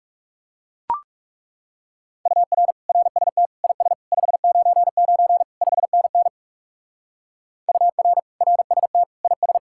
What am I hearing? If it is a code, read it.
Morse code, heard as a keyed tone, about 37 wpm, 700 Hz. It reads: UR RST IS 599 5NN UR RST IS
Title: Speed-Racing Common QSO Phrases - 30wpm